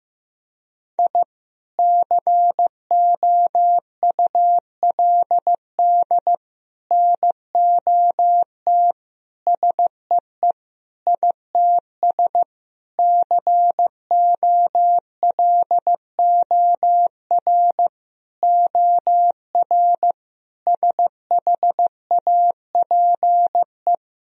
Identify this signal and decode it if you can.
Morse code, heard as a keyed tone, about 15 wpm, 700 Hz. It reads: I COULD NOT SEE ITS COLOR OR SHAPE
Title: Sentences from Top 1000 Words - Review Part 3 - 15wpm